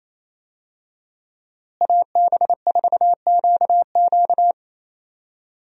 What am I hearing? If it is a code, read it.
Morse code, heard as a keyed tone, about 28 wpm, 700 Hz. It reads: AB4QQ
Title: USA Call Signs - 5 Characters - Repeated 2x - 28wpm